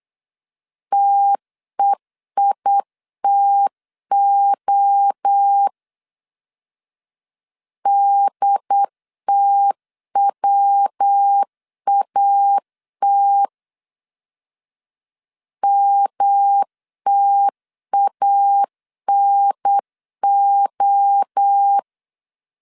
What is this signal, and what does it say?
Morse code, heard as a keyed tone, about 8 wpm, 800 Hz. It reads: TEITO DTWAT MTANO
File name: CW / DET / 1 3 1500C S B 8wpm 4141